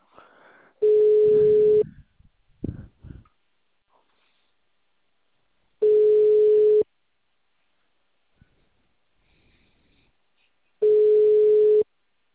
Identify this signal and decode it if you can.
Ringback tone (Europe, 425 Hz).
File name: out- -2055- - - .85500